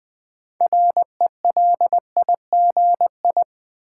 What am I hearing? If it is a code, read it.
Morse code, heard as a keyed tone, about 20 wpm, 700 Hz. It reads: RELIGI
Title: Build A Word - 3 Letters At A Time - 9-Letter Words - 20wpm